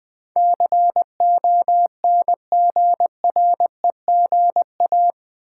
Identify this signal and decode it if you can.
Morse code, heard as a keyed tone, about 20 wpm, 700 Hz. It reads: CONGREGA
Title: Build A Word - 4 Letters At A Time - 12-Letter Words - 20wpm